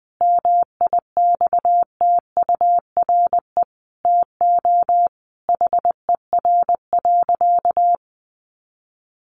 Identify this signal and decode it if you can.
Morse code, heard as a keyed tone, about 20 wpm, 700 Hz. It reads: MIXTURE TO HER.
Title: Sentences from Top 1500 Words 20wpm